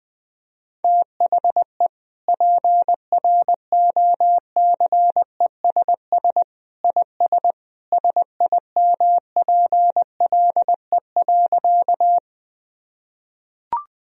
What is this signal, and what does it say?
Morse code, heard as a keyed tone, about 20 wpm, 700 Hz. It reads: THE PROCESS IS SIMPLE.
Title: Sentences from Top 1000 Words - Review Part 3 - 20wpm